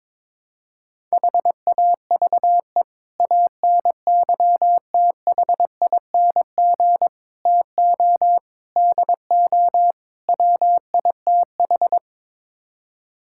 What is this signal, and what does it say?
Morse code, heard as a keyed tone, about 22 wpm, 700 Hz. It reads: HAVE ANYTHING TO DO WITH
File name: practice / 22wpm / Build A Phrase - 5-Word Phrases - 22wpm